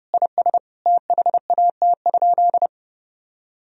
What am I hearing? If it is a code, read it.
Morse code, heard as a keyed tone, about 30 wpm, 700 Hz. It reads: IS THAT?
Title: Sentences from Top 1000 Words - Review Part 3 - 30wpm